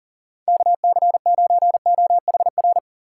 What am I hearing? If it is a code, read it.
Morse code, heard as a keyed tone, about 40 wpm, 700 Hz. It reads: KC9OHR
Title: USA Call Signs - 6 Characters - 4x Character Spacing - 40wpm